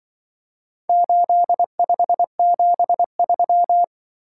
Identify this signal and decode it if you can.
Morse code, heard as a keyed tone, about 24 wpm, 700 Hz. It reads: 8573
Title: Speed-Racing - Four Letter-Number Combinations - 20wpm